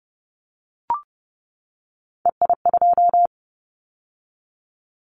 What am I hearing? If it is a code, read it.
Morse code, heard as a keyed tone, about 30 wpm, 700 Hz. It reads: EI2
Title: Callsign Prefixes for Top 50 DXCC - 30wpm